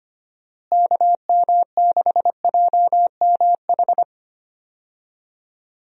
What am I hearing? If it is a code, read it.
Morse code, heard as a keyed tone, about 25 wpm, 700 Hz. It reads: KM6JMH
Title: USA Call Signs - 6 Characters - Repeated 3x - 25wpm